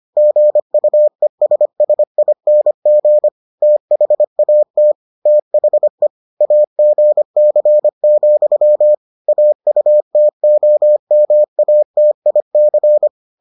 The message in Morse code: GUESSING THAT THE AGC, AUTOMATIC

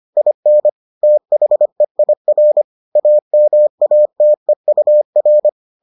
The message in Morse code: IN THEIR AMATEUR

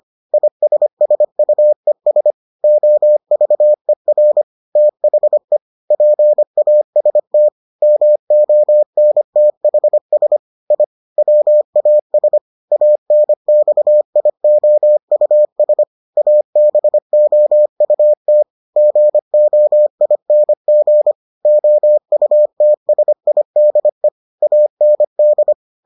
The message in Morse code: ISSUES OVER THE PAST MONTHS I WAS ANXIOUS ABOUT GOING OUTSIDE AND

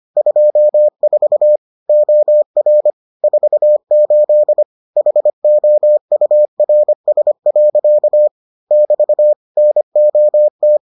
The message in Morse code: 24 OR 48 HOURS. <BT> NOT

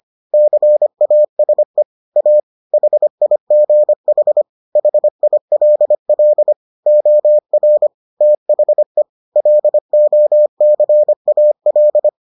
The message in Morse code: CASE A HIGH HILL OR THE LOCAL

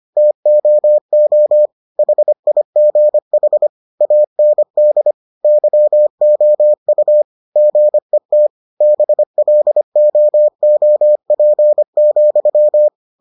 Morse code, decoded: TOO HIGH AND YOU GET BLOOP,

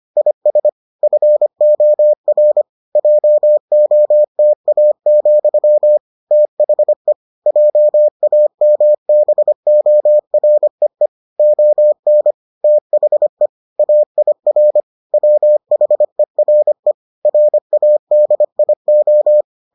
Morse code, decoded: IS FOR JOTA, THE JAMBOREE ON THE AIR WHERE RADIO

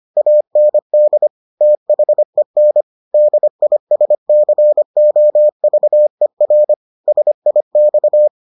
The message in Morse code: AND THEN DISCOVER SIX

